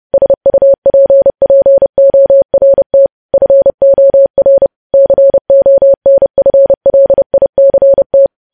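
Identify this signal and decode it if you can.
Morse code, heard as a keyed tone, about 30 wpm, 550 Hz. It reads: SUPPORT FOR CONFLICT
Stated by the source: Morse code news headlines sent at 30, preceded by vvv vvv.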